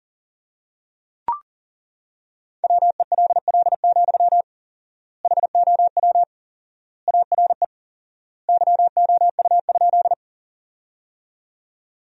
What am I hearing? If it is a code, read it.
Morse code, heard as a keyed tone, about 40 wpm, 700 Hz. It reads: WELL, HOW ARE YOU?
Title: 4x Word-Spacing - Sentences from Top 200 Words - 40wpm